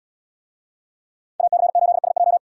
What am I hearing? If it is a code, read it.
Morse code, heard as a keyed tone, about 77 wpm, 700 Hz. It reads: AC8SJ